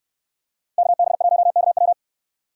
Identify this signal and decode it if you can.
Morse code, heard as a keyed tone, about 69 wpm, 700 Hz. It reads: KB1KW